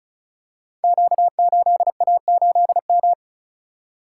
Morse code, heard as a keyed tone, about 36 wpm, 700 Hz. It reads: Q8A8M
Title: Code Groups - 5 Letter:Numbers - 35wpm